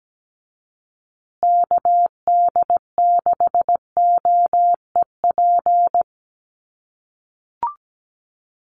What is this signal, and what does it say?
Morse code, heard as a keyed tone, about 17 wpm, 700 Hz. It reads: KD6OEP
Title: USA Call Signs Repeated 2x 17wpm